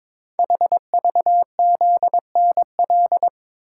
Morse code, heard as a keyed tone, about 22 wpm, 700 Hz. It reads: HVZNL
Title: Code Groups - 5 Letters - 22wpm